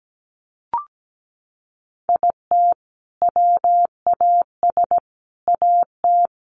Morse code, heard as a keyed tone, about 17 wpm, 700 Hz. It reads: IT WAS AT
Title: Sentences from Top 1500 Words 17wpm